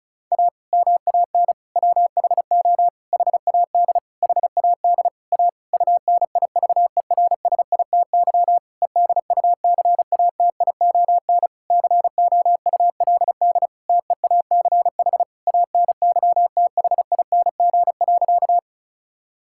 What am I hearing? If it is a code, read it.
Morse code, heard as a keyed tone, about 34 wpm, 700 Hz. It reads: A MAN WHO HAD HAD A UNIVERSITY EDUCATION COULD TEACH ANYTHING.
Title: Sentences from Top 1000 Words - Review Part 6 - 35wpm